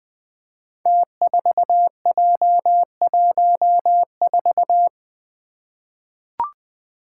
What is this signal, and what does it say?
Morse code, heard as a keyed tone, about 20 wpm, 700 Hz. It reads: T4J14
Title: Five Challenging Letter-Number Combinations 20wpm